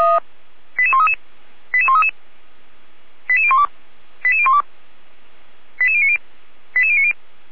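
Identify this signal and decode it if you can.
Five-tone selective call (ZVEI), heard as quick runs of five tones, tones of about 70 ms. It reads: 88133 88133 88012 88012 88090 88090
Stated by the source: ZVEI five-tone selective call 88241 received on SDR.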